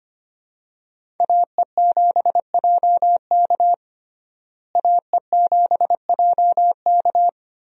Morse code, heard as a keyed tone, about 25 wpm, 700 Hz. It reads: AE7JK AE7JK
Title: USA Call Signs - 5 Characters - Repeated 2x - 25wpm